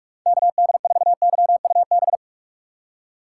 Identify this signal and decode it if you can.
Morse code, heard as a keyed tone, about 54 wpm, 700 Hz. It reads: KD4YUB
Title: Speed Racing - USA 2x3 Call Signs 45wpm